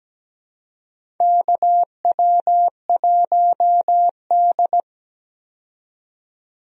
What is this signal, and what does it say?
Morse code, heard as a keyed tone, about 17 wpm, 700 Hz. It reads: KW1D
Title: USA Call Signs - 4 Characters - 17wpm